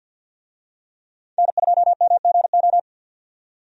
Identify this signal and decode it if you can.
Morse code, heard as a keyed tone, about 50 wpm, 700 Hz. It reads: N2MGO
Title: USA Call Signs - 5 Characters - 50wpm